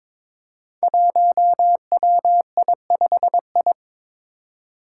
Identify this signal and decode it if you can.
Morse code, heard as a keyed tone, about 22 wpm, 700 Hz. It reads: 1WI5I
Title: Five Challenging Letter-Number Combinations 22wpm